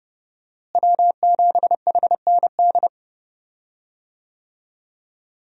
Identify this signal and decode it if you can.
Morse code, heard as a keyed tone, about 30 wpm, 700 Hz. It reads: W7HND
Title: USA Call Signs - 5 Characters - Repeated 2x - 30wpm